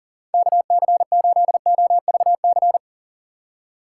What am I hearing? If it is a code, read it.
Morse code, heard as a keyed tone, about 40 wpm, 700 Hz. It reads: KC8OVC
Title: USA Call Signs - 6 Characters - 3x Character Spacing - 40wpm